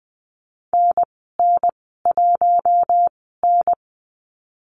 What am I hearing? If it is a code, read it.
Morse code, heard as a keyed tone, about 20 wpm, spaced as for 14 wpm, 700 Hz. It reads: NN1N
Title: 082 Lesson 14g - Number 2 - Call Signs 20wpm